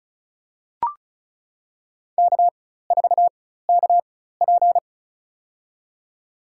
Morse code, heard as a keyed tone, about 36 wpm, spaced as for 16 wpm, 700 Hz. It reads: K4KP
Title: USA Call Signs - 4 Characters - 4x Character Spacing - 35wpm